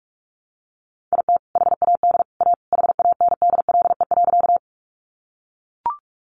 Morse code, heard as a keyed tone, about 44 wpm, 700 Hz. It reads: IT HAD A HANDLE.
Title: Sentences from Top 1100 Words 45wpm